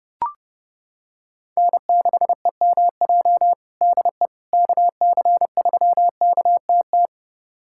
Morse code, heard as a keyed tone, about 30 wpm, 700 Hz. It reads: N6EMJ DE KC3KTT
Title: USA Callsign DE Callsign 30wpm